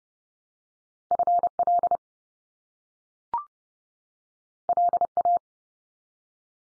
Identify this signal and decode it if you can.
Morse code, heard as a keyed tone, about 30 wpm, 700 Hz. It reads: FL LA
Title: US States 30WPM